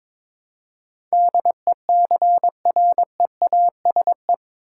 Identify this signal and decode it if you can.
Morse code, heard as a keyed tone, about 22 wpm, 700 Hz. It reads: DECREASE
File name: practice / 22wpm / Eight Letter Words - 22wpm